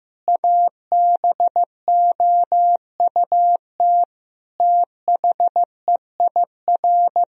Morse code, heard as a keyed tone, about 15 wpm, 700 Hz. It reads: ABOUT THEIR
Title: Sentences from Top 1000 Words - Review Part 4 - 15wpm